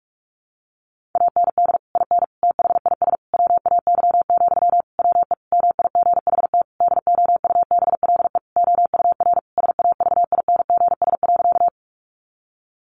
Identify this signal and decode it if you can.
Morse code, heard as a keyed tone, about 44 wpm, 700 Hz. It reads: AND IN THIS WAY, WE MIGHT DOUBLE OUR SAVINGS.
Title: Sentences from Top 1400 Words 45wpm